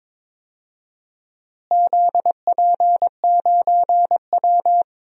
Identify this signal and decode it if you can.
Morse code, heard as a keyed tone, about 21 wpm, 700 Hz. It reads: ZP9W
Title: DXCC-100 Call Signs - 4 Characters - Repeated 3x - 22wpm